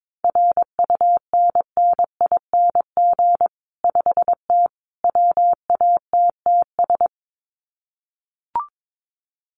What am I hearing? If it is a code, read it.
Morse code, heard as a keyed tone, about 22 wpm, 700 Hz. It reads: RUNNING 5T WATTS
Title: Common QSO Phrases 22wpm